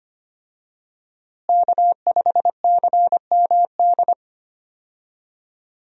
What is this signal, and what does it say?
Morse code, heard as a keyed tone, about 25 wpm, 700 Hz. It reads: K5CMD
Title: USA Call Signs - 5 Characters - Repeated 2x - 25wpm